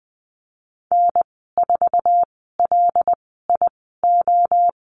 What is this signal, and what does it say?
Morse code, heard as a keyed tone, about 20 wpm, spaced as for 15 wpm, 700 Hz. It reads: N4LIO
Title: 076 Lesson 13g - Letter L - Call Signs 20wpm